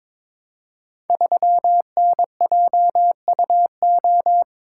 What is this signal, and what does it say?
Morse code, heard as a keyed tone, about 21 wpm, 700 Hz. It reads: 3NJUO
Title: Code Groups - 5 Letter:Numbers - 22wpm